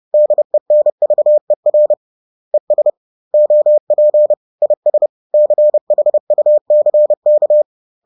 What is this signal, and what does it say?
Morse code, heard as a keyed tone, about 30 wpm, 600 Hz. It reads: DENVER ES OP IS CHUCK
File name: 30wpm